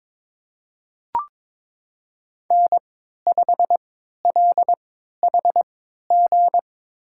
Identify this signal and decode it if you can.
Morse code, heard as a keyed tone, about 23 wpm, spaced as for 13 wpm, 700 Hz. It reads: N5LHG
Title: USA Call Signs - 5 Characters - 3x Character Spacing - 22wpm